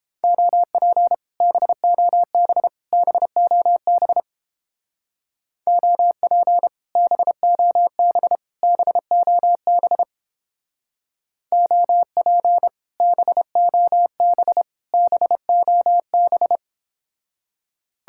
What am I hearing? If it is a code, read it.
Morse code, heard as a keyed tone, about 31 wpm, 700 Hz. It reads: OP BOB BOB OP BOB BOB OP BOB BOB